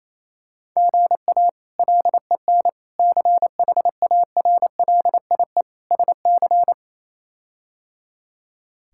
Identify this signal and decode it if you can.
Morse code, heard as a keyed tone, about 27 wpm, 700 Hz. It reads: GA LEN CHARLIE SC
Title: K1USN SST QSO Practice - 28wpm